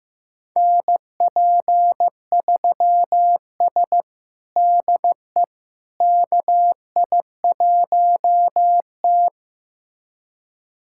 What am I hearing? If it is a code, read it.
Morse code, heard as a keyed tone, about 16 wpm, 700 Hz. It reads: NP3S DE KI1T
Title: USA Callsign DE Callsign - 4 Characters - 15wpm